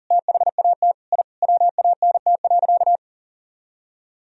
Morse code, heard as a keyed tone, about 40 wpm, 700 Hz. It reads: THAT I WANT.
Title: Sentences from Top 1000 Words - Review Part 3 - 40wpm